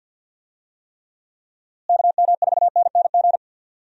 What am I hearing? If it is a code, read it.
Morse code, heard as a keyed tone, about 53 wpm, 700 Hz. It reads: KM4NNG
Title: USA Call Signs - 6 Characters - Repeated 2x - 50wpm